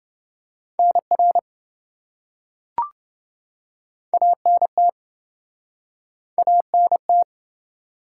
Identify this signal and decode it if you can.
Morse code, heard as a keyed tone, about 27 wpm, 700 Hz. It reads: NR ANT ANT